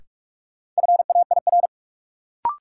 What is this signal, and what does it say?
Morse code, heard as a keyed tone, about 46 wpm, 700 Hz. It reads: FAIR
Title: Sentences from Top 1300 Words - New Word Drill Alt - 45wpm